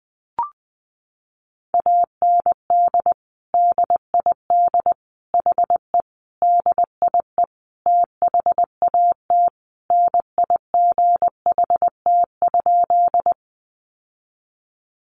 Morse code, heard as a keyed tone, about 20 wpm, 700 Hz. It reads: AND DID HE DIE THAT NIGHT?
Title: Sentences from Top 1300 Words 20wpm